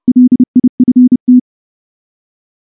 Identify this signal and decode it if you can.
Morse code, heard as a keyed tone, about 28 wpm, 250 Hz. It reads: LIFT
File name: morse / morse-training-30WPM